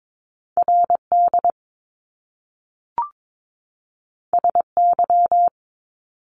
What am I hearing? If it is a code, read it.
Morse code, heard as a keyed tone, about 22 wpm, 700 Hz. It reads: RD SY
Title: Top Two Letter Combinations in Words 22wpm